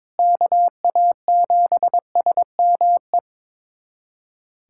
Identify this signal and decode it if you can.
Morse code, heard as a keyed tone, about 22 wpm, 700 Hz. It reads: KA7SME
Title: USA Callsign DE Callsign - 6 Characters - 22wpm